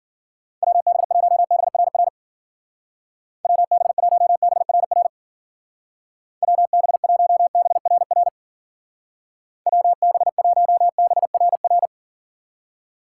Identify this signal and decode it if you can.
Morse code, heard as a keyed tone, about 49 wpm, 700 Hz. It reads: WB1BRR WB1BRR WB1BRR WB1BRR
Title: Speed Racing - USA 2x3 Call Signs 40wpm